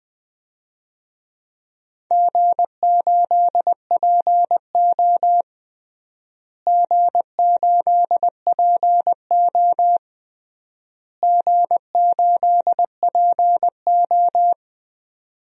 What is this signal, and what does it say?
Morse code, heard as a keyed tone, about 20 wpm, 700 Hz. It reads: G8PO G8PO G8PO
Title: DXCC-100 Call Signs - 4 Characters - Repeated 3x - 20wpm